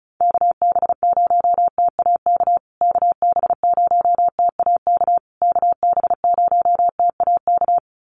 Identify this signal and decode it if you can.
Morse code, heard as a keyed tone, about 34 wpm, 700 Hz. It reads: KB0TAK KB0TAK KB0TAK
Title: USA Call Signs Repeated 3x 35wpm